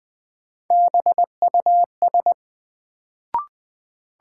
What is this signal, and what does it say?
Morse code, heard as a keyed tone, about 20 wpm, 700 Hz. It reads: BUS
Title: Sentences from Top 1000 Words - Review Part 6 - New Word Drill - 20wpm